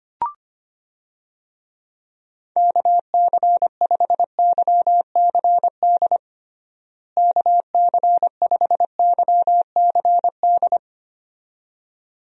Morse code, heard as a keyed tone, about 25 wpm, 700 Hz. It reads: KC5YCD KC5YCD
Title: USA Call Signs - 6 Characters - Repeated 2x - 25wpm